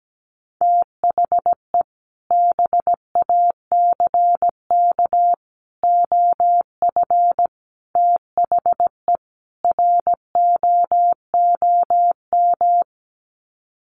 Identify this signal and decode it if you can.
Morse code, heard as a keyed tone, about 17 wpm, 700 Hz. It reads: THE BACK OF THE ROOM
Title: Sets of 5 Words Encore 17wpm